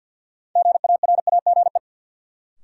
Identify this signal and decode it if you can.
Morse code, heard as a keyed tone, about 50 wpm, 700 Hz. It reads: GARAGE
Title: Sentences from Top 1600 Words - New Word Drill - 50wpm